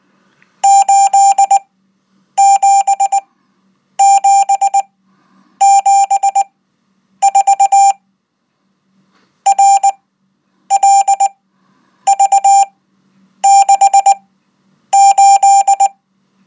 Morse code, heard as a keyed tone, about 19 wpm, spaced as for 8 wpm, 790 Hz. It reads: 87774 RLV68